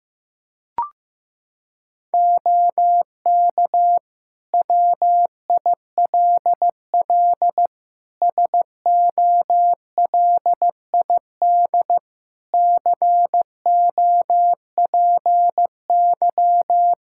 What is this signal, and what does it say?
Morse code, heard as a keyed tone, about 16 wpm, 700 Hz. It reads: OK WILL SOLID COPY
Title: Practice QSO - Exchange 3 - More Info - 15wpm